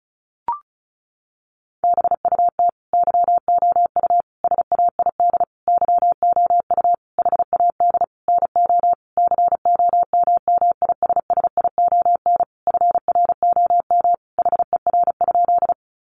Morse code, heard as a keyed tone, about 34 wpm, 700 Hz. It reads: BUT YOU SAID YOU HAD NO COMMISSION FROM HER?
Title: Sentences from Top 1400 Words 35wpm